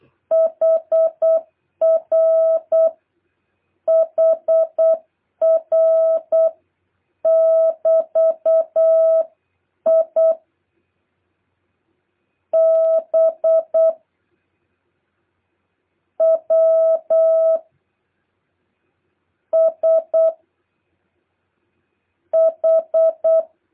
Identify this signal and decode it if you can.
Morse code, heard as a keyed tone, about 8 wpm, 650 Hz. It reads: HR HR <BT> I B W S H
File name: jh8cbh / 2023cw